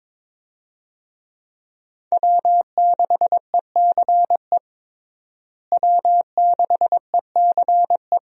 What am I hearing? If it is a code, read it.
Morse code, heard as a keyed tone, about 22 wpm, 700 Hz. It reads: W6ECE W6ECE
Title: USA Call Signs - 5 Characters - Repeated 2x - 22wpm